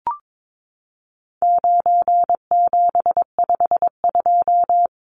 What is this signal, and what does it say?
Morse code, heard as a keyed tone, about 22 wpm, 700 Hz. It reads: 9752
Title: Numbers - 4-Digit 22wpm